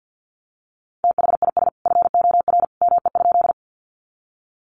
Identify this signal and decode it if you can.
Morse code, heard as a keyed tone, about 50 wpm, 700 Hz. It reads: THIS FOR ME?
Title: Sentences from Top 200 Words 50wpm